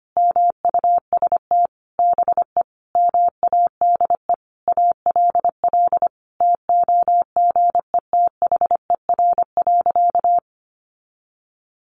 Morse code, heard as a keyed tone, about 25 wpm, 700 Hz. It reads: MUST BE MADE ALL TOGETHER.
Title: Sentences from Top 1100 Words 25wpm